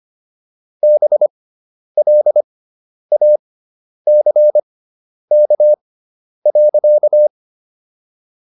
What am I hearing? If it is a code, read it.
Morse code, heard as a keyed tone, about 25 wpm, spaced as for 10 wpm, 600 Hz. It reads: BLACK.